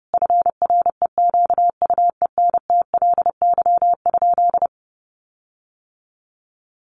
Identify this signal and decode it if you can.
Morse code, heard as a keyed tone, about 30 wpm, 700 Hz. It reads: FREQUENTLY?
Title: Sentences from Top 1300 Words 30wpm